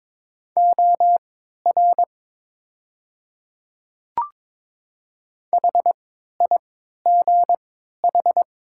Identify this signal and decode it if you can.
Morse code, heard as a keyed tone, about 22 wpm, spaced as for 13 wpm, 700 Hz. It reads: OR HIGH
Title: 3x Character Spacing - Top 200 Words - 22wpm